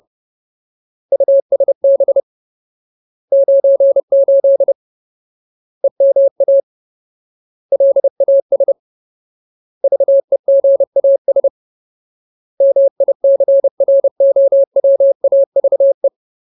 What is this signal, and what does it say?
Morse code, heard as a keyed tone, about 30 wpm, 550 Hz. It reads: USB 98 EMA LAS VEGAS MICROWAVE
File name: Ultimate Word List / 30WPM Ultimate